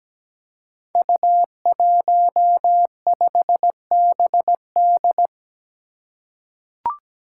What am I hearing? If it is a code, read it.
Morse code, heard as a keyed tone, about 17 wpm, 700 Hz. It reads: U15BD
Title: Five Challenging Letter-Number Combinations 17wpm